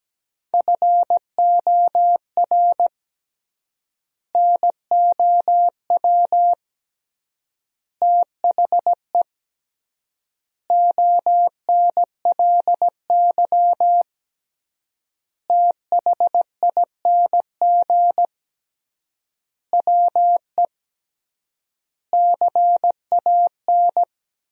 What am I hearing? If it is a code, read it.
Morse code, heard as a keyed tone, about 17 wpm, 700 Hz. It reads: FOR NOW THE ONLY THING WE CAN
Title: 3x Word-Spacing - Sentences from Top 200 Words - 17wpm